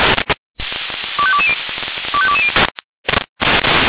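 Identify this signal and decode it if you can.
Five-tone selective call (ZVEI), heard as quick runs of five tones, tones of about 70 ms. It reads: 25220 25220
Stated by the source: ZVEI five-tone call received by SDR.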